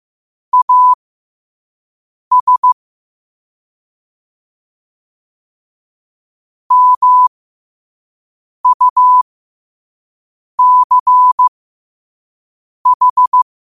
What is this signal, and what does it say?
Morse code, heard as a keyed tone, about 15 wpm, spaced as for 5 wpm, 1000 Hz. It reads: AS MUCH
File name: 5wpm